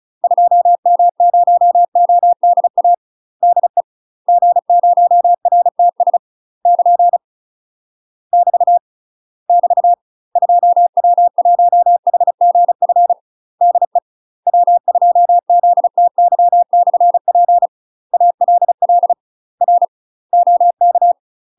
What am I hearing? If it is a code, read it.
Morse code, heard as a keyed tone, about 34 wpm, 700 Hz. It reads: 2M0ODA DE G0RTS <KN> <BT> <BT> 2W1HGF DE W2ZTY/P ALL R OK
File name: UK MASTER QSOs TRAINING / 35WPM / UK MASTER QSOs 0024